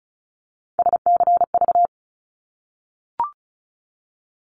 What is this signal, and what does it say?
Morse code, heard as a keyed tone, about 36 wpm, 700 Hz. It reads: SCV
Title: ARRL Sections 35wpm